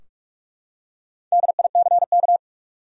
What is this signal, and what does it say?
Morse code, heard as a keyed tone, about 46 wpm, 700 Hz. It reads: DICK
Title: USA Common Names - Max 5 Letters - 45wpm